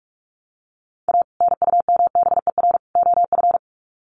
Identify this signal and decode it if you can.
Morse code, heard as a keyed tone, about 44 wpm, 700 Hz. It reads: A NUMBER OF
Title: Sets of 3 Words 45wpm